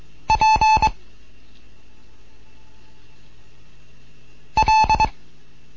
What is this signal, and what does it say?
Morse code, heard as a keyed tone, about 23 wpm, 870 Hz. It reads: P L